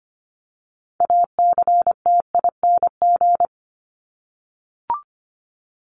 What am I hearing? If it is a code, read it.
Morse code, heard as a keyed tone, about 25 wpm, 700 Hz. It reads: ACTING
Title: ING Suffix - Top 100 Words with Base Words 25wpm